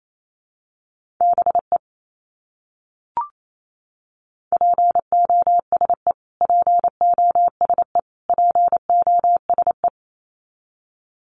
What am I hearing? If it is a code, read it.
Morse code, heard as a keyed tone, about 28 wpm, 700 Hz. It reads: BE POSE POSE POSE
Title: Top 500 Words Repeated 28wpm